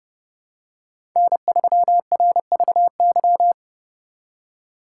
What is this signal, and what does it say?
Morse code, heard as a keyed tone, about 28 wpm, 700 Hz. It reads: N3RVY